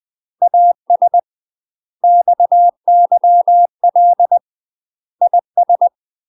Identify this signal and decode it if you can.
Morse code, heard as a keyed tone, about 20 wpm, 700 Hz. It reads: AS XYL IS
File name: UK MASTER QSOs TRAINING/20WPM